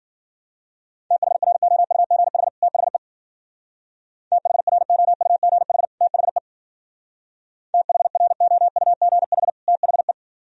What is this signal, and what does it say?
Morse code, heard as a keyed tone, about 50 wpm, 700 Hz. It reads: THROUGH THE THROUGH THE THROUGH THE